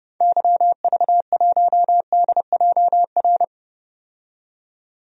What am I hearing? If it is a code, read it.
Morse code, heard as a keyed tone, about 30 wpm, 700 Hz. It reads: YV1DJR
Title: DXCC-100 Call Signs - 6 Characters - 30wpm